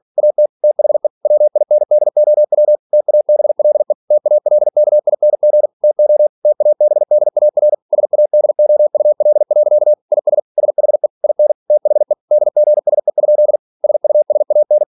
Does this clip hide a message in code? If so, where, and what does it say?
Yes: Morse code from 0.2 to 14.8 s, heard as a keyed tone, about 46 wpm, 600 Hz. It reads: AT THE WINDOW TABLE TALKING TO TABDAR SADOUL. IS SHE IN THE DGSE? SUSAN